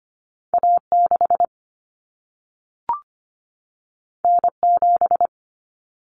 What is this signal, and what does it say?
Morse code, heard as a keyed tone, about 25 wpm, 700 Hz. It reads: A6 N7
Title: Common Cut Numbers - 2-Digit 25wpm